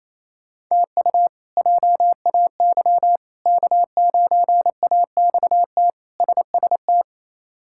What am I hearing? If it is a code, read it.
Morse code, heard as a keyed tone, about 28 wpm, 700 Hz. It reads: TU JAY K9AXT SST
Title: K1USN SST QSO Practice - 28wpm